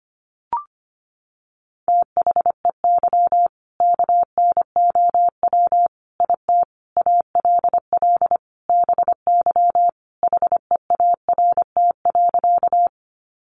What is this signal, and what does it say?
Morse code, heard as a keyed tone, about 25 wpm, 700 Hz. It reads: THEY KNOW IT ALL BY HEART.
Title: Sentences from Top 1200 Words 25wpm